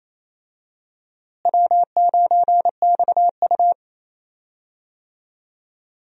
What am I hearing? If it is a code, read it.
Morse code, heard as a keyed tone, about 28 wpm, 700 Hz. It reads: W9XU
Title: USA Call Signs - 4 Characters - Repeated 2x - 28wpm